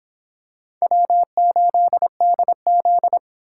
Morse code, heard as a keyed tone, about 26 wpm, 700 Hz. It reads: W8DZ